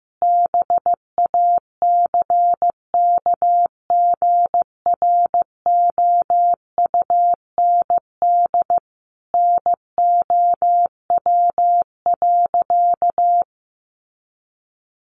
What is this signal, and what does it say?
Morse code, heard as a keyed tone, about 15 wpm, 700 Hz. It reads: BACKGROUND NOW.
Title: Sentences from Top 1300 Words 15wpm